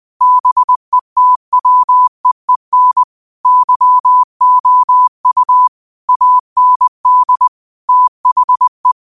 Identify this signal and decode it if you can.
Morse code, heard as a keyed tone, about 20 wpm, 1000 Hz. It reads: BETWEEN YOU AND THE
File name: qotdfiles / 20wpm